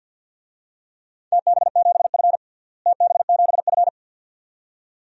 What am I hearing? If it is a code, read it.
Morse code, heard as a keyed tone, about 50 wpm, 700 Hz. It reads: TB7F TB7F
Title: DXCC-100 Call Signs - 4 Characters - Repeated 2x - 50wpm